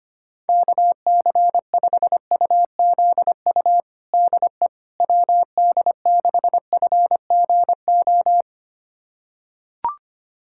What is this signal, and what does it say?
Morse code, heard as a keyed tone, about 25 wpm, 700 Hz. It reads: KC5UZU DE WD6FGO
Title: USA Callsign DE Callsign 25wpm